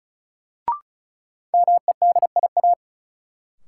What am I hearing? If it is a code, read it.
Morse code, heard as a keyed tone, about 36 wpm, 700 Hz. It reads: MEDIA
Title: Sentences from Top 1000 Words - Review Part 4 - New Word Drill - 35wpm